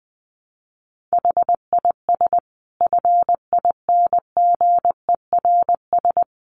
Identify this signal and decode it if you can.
Morse code, heard as a keyed tone, about 19 wpm, 700 Hz. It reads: HIS FINGERS
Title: A Sound Of Thunder 20wpm